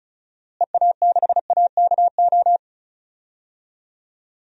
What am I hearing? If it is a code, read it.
Morse code, heard as a keyed tone, about 36 wpm, 700 Hz. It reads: EA6AKO
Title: DXCC-100 Call Signs - 6 Characters - Repeated 3x - 35wpm